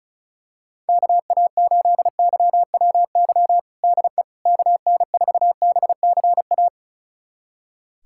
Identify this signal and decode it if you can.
Morse code, heard as a keyed tone, about 36 wpm, 700 Hz. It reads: KA8YWY DE KN4BCA
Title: USA Callsign DE Callsign - 6 Characters - 35wpm